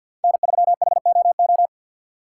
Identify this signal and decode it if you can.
Morse code, heard as a keyed tone, about 47 wpm, 700 Hz. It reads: N3HOO
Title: USA Call Signs - 5 Characters - 6x Character Spacing - 50wpm